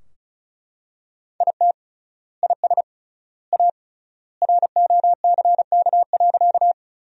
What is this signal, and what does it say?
Morse code, heard as a keyed tone, about 36 wpm, 700 Hz. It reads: IT IS A ROCK.
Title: 3x Word-Spacing - Sentences from Top 400 Words - 35wpm